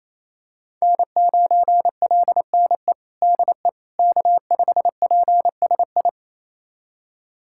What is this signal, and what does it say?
Morse code, heard as a keyed tone, about 28 wpm, 700 Hz. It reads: N9LNE DE K5PSI
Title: USA Callsign DE Callsign - 5 Characters - 28wpm